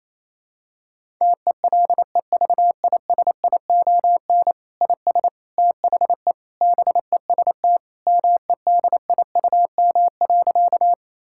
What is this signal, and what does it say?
Morse code, heard as a keyed tone, about 28 wpm, 700 Hz. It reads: TELEVISION IS THE BEST MEDIUM.
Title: Sentences from Top 1000 Words - Review Part 6 - 28wpm